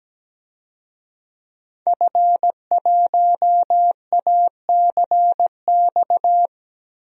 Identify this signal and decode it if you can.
Morse code, heard as a keyed tone, about 17 wpm, 700 Hz. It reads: F1ACX
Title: DXCC-100 Call Signs - 5 Characters - Repeated 3x - 17wpm